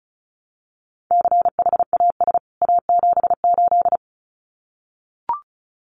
Morse code, heard as a keyed tone, about 34 wpm, 700 Hz. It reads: CHAS A78
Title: CWT Exchange 35wpm